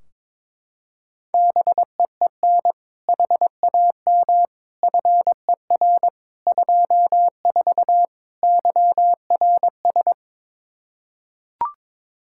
Morse code, heard as a keyed tone, about 21 wpm, 700 Hz. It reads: BEEN HAM FER 24 YRS
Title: Practice QSO - Exchange 3 - More Info - 22wpm